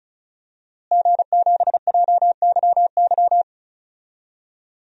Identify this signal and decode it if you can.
Morse code, heard as a keyed tone, about 34 wpm, 700 Hz. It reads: G7JYY
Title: DXCC-100 Call Signs - 5 Characters - Repeated 2x - 35wpm